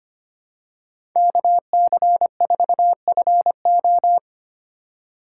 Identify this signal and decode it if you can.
Morse code, heard as a keyed tone, about 25 wpm, 700 Hz. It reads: KC4FO
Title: USA Call Signs - 5 Characters - 3x Character Spacing - 25wpm